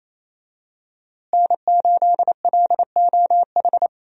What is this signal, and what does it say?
Morse code, heard as a keyed tone, about 28 wpm, 700 Hz. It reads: N8LOH
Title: USA Call Signs - 5 Characters - Repeated 2x - 28wpm